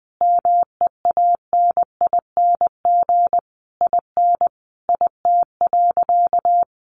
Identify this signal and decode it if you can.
Morse code, heard as a keyed tone, about 20 wpm, 700 Hz. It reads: MEANING IN IT.
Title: Sentences from Top 1200 Words 20wpm